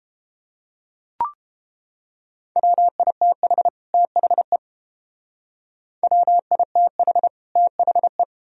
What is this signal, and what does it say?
Morse code, heard as a keyed tone, about 30 wpm, 700 Hz. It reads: WITH THE WITH THE